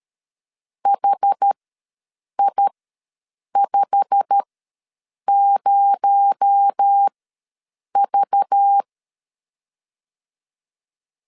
Morse code, heard as a keyed tone, about 13 wpm, spaced as for 7 wpm, 800 Hz. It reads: HI50V
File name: LEZIONI / 9 28 ws 12wpm 3354